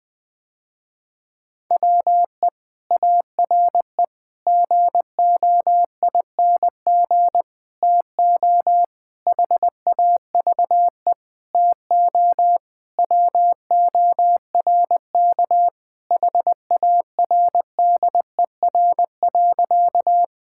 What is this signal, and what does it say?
Morse code, heard as a keyed tone, about 20 wpm, 700 Hz. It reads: WE ARE GOING TO HAVE TO WORK HARDER.
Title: Sentences with 6-Word Phrases - Spoken Beginning - 20wpm